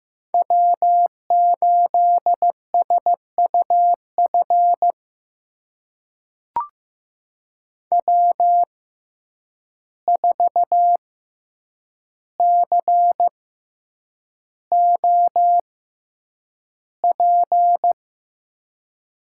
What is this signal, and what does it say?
Morse code, heard as a keyed tone, about 15 wpm, spaced as for 8 wpm, 700 Hz. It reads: W8SUF W4COP
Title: USA Call Signs - 5 Characters - 6x Character Spacing - 15wpm